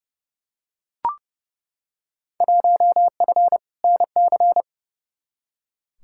Morse code, heard as a keyed tone, about 30 wpm, 700 Hz. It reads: 1F NC
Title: Field Day Exchanges - 30wpm